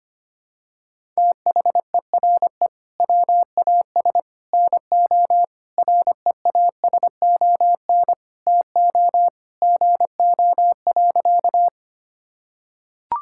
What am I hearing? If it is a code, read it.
Morse code, heard as a keyed tone, about 25 wpm, 700 Hz. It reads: THERE WAS NO REASON TO GO.
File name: practice / 25wpm / Sentences with 3-Word Phrases - 25wpm